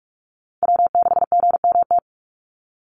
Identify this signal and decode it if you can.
Morse code, heard as a keyed tone, about 44 wpm, 700 Hz. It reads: W6GMT
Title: Call Signs - 45wpm